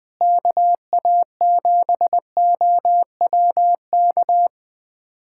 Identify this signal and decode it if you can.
Morse code, heard as a keyed tone, about 20 wpm, 700 Hz. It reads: KA7OWK
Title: USA Callsign DE Callsign 20wpm